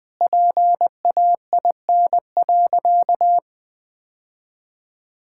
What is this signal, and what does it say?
Morse code, heard as a keyed tone, about 20 wpm, 700 Hz. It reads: PAIN.
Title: Sentences from Top 1600 Words 20wpm